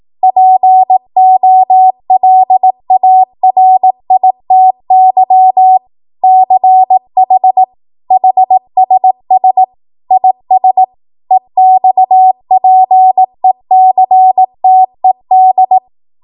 Morse code, heard as a keyed tone, about 18 wpm, 750 Hz. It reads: POLARITY CH HSS IS EXPECTED
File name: Morse / 18WPM